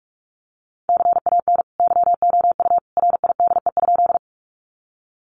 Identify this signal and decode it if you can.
Morse code, heard as a keyed tone, about 44 wpm, 700 Hz. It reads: CAN YOU RIDE?
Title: Sentences from Top 1200 Words 45wpm